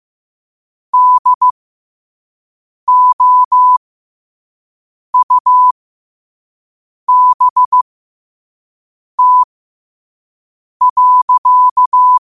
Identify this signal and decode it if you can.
Morse code, heard as a keyed tone, about 15 wpm, spaced as for 5 wpm, 1000 Hz. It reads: DOUBT.
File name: qotdfiles / 5wpm